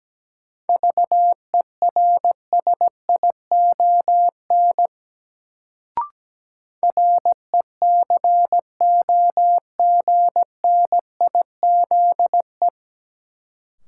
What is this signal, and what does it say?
Morse code, heard as a keyed tone, about 17 wpm, 700 Hz. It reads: VERSION RECOGNIZE
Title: Sentences from Top 1400 Words - New Word Drill - 17wpm